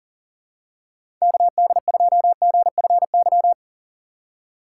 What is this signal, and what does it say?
Morse code, heard as a keyed tone, about 40 wpm, 700 Hz. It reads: KD2GFY
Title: USA Call Signs - 6 Characters - 4x Character Spacing - 40wpm